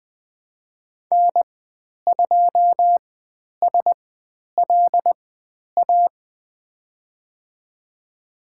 Morse code, heard as a keyed tone, about 20 wpm, spaced as for 10 wpm, 700 Hz. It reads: N2SLA
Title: 095 Lesson 16g - Letter c - Call Signs 20wpm